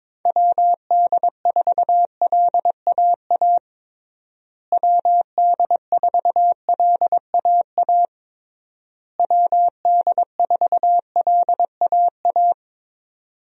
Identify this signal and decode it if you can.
Morse code, heard as a keyed tone, about 22 wpm, 700 Hz. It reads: WD4LAA WD4LAA WD4LAA
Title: USA Call Signs - 6 Characters - Repeated 3x - 22wpm